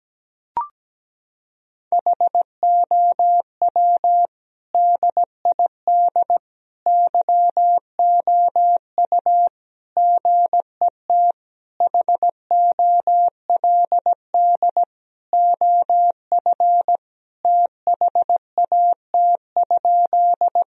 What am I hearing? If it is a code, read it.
Morse code, heard as a keyed tone, about 17 wpm, 700 Hz. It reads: HOW DID YOU GET HOLD OF THAT?
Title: Sentences from Top 1000 Words - Review Part 2 - 17wpm